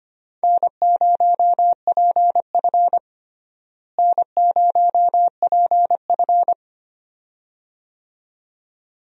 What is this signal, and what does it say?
Morse code, heard as a keyed tone, about 25 wpm, 700 Hz. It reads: N0PF N0PF
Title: USA Call Signs - 4 Characters - Repeated 2x - 25wpm